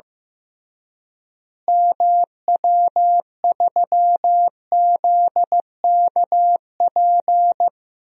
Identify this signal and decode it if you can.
Morse code, heard as a keyed tone, about 16 wpm, 700 Hz. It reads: MW3ZKP
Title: DXCC-100 Call Signs - 6 Characters - Repeated 2x - 15wpm